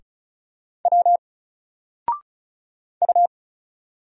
Morse code, heard as a keyed tone, about 34 wpm, 700 Hz. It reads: W U
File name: Single-Letter-Number-35wpm